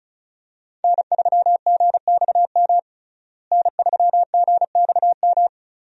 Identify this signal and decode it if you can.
Morse code, heard as a keyed tone, about 36 wpm, 700 Hz. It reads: N3GXM N3GXM
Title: USA Call Signs - 5 Characters - Repeated 2x - 35wpm